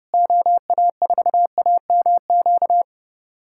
Morse code, heard as a keyed tone, about 30 wpm, 700 Hz. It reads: OA4AMQ
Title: USA Callsign DE Callsign - 6 Characters - 30wpm